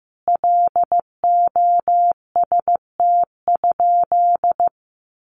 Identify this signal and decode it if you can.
Morse code, heard as a keyed tone, about 15 wpm, 700 Hz. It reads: LOST?
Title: Sentences from Top 1400 Words 15wpm